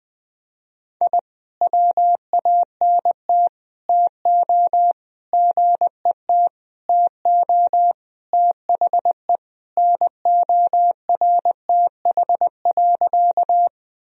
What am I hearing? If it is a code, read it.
Morse code, heard as a keyed tone, about 20 wpm, 700 Hz. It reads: I WANT TO GET TO THE NORTH.
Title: Sentences from Top 1000 Words - Review Part 1 - 20wpm